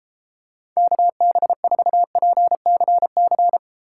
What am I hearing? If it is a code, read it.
Morse code, heard as a keyed tone, about 32 wpm, 700 Hz. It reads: KB4PCC